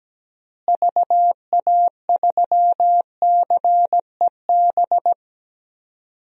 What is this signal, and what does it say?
Morse code, heard as a keyed tone, about 17 wpm, 700 Hz. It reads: VA3CEB
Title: DXCC-100 Call Signs - 6 Characters - Repeated 2x - 17wpm